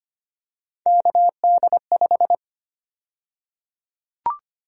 Morse code, heard as a keyed tone, about 25 wpm, 700 Hz. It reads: KD5
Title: USA Call Sign Prefixes - Repeated 2x - 25wpm